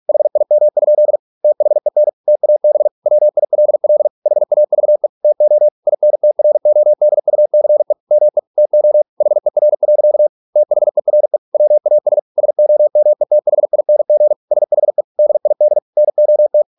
Morse code, heard as a keyed tone, about 46 wpm, 600 Hz. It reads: HIM? THEN TAB WILL HAVE TO INTRODUCE ME TO HER. THERE WAS SOMETHING SHE DID NOT